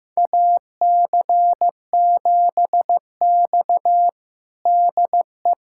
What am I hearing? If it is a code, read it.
Morse code, heard as a keyed tone, about 15 wpm, 700 Hz. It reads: AC7X DE
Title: USA Callsign DE Callsign - 4 Characters - 15wpm